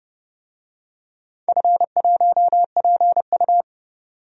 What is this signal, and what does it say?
Morse code, heard as a keyed tone, about 30 wpm, 700 Hz. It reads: F1PU
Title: DXCC-100 Call Signs - 4 Characters - 30wpm